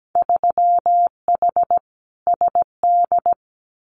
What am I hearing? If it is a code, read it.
Morse code, heard as a keyed tone, about 17 wpm, 700 Hz. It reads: 3H SD
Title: Winter Field Day Exchanges Encore 17wpm